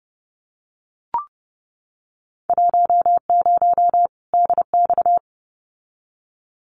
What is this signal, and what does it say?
Morse code, heard as a keyed tone, about 30 wpm, 700 Hz. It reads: 10 DX
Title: Winter Field Day Exchanges 30wpm